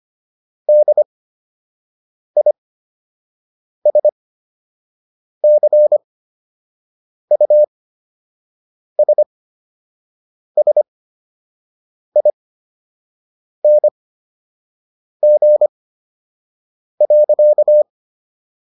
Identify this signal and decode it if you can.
Morse code, heard as a keyed tone, about 25 wpm, spaced as for 6 wpm, 600 Hz. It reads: DISCUSSING.